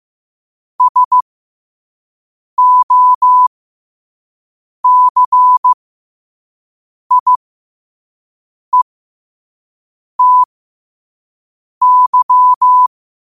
Morse code, heard as a keyed tone, about 15 wpm, spaced as for 5 wpm, 1000 Hz. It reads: SOCIETY
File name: qotdfiles / 5wpm